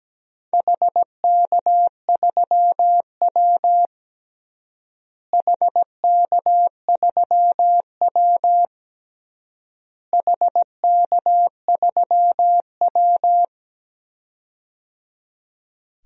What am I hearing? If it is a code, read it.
Morse code, heard as a keyed tone, about 17 wpm, 700 Hz. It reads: HK3W HK3W HK3W
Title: DXCC-100 Call Signs - 4 Characters - Repeated 3x - 17wpm